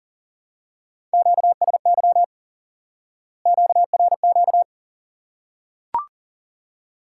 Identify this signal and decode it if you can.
Morse code, heard as a keyed tone, about 40 wpm, 700 Hz. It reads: QSY QRQ
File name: IQE Course 3 - Lesson 89 - YAGI - Two Elements - 40wpm